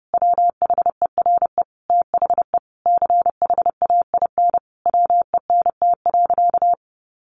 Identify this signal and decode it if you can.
Morse code, heard as a keyed tone, about 30 wpm, 700 Hz. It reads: WHERE THE CHAIN WENT.
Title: Sentences from Top 1500 Words 30wpm